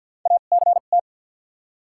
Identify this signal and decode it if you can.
Morse code, heard as a keyed tone, about 48 wpm, 700 Hz. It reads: A C T
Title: 2x Character Spacing - Top 200 Words - 50wpm